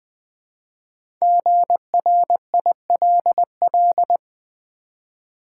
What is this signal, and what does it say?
Morse code, heard as a keyed tone, about 20 wpm, 700 Hz. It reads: GRILL
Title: Easy Words with Hints - 20wpm